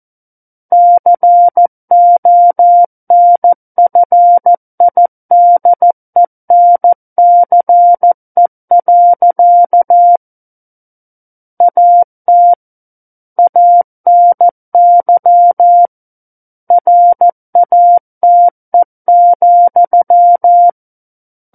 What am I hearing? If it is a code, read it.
Morse code, heard as a keyed tone, about 14 wpm, 700 Hz. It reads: CONFIDENCE. AT ANY RATE,
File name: Book One / 05-Chapter 5 - 14 WPM